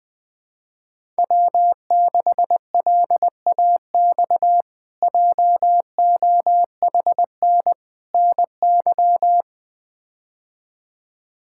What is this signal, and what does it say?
Morse code, heard as a keyed tone, about 20 wpm, 700 Hz. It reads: W6LAX JOHN NY
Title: K1USN SST QSO Practice - 20wpm